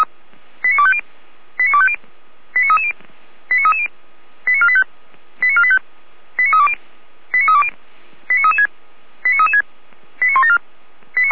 Five-tone selective call (ZVEI), heard as quick runs of five tones, tones of about 70 ms. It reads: 89250 89250 89330 89330 89485 89485 89320 89320 89336 89336 89174